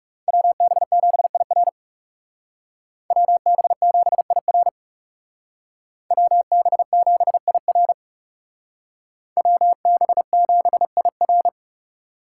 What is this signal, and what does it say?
Morse code, heard as a keyed tone, about 36 wpm, 700 Hz. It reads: WB7IR WB7IR WB7IR WB7IR
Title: Speed Racing - USA 2x2 Call Signs 30wpm